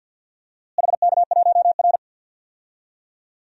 Morse code, heard as a keyed tone, about 52 wpm, 700 Hz. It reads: HK1R
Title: DXCC-100 Call Signs - 4 Characters - Repeated 3x - 50wpm